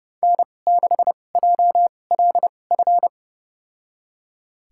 Morse code, heard as a keyed tone, about 30 wpm, spaced as for 22 wpm, 700 Hz. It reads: N6JLF
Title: USA Call Signs - 5 Characters - 2x Character Spacing - 30wpm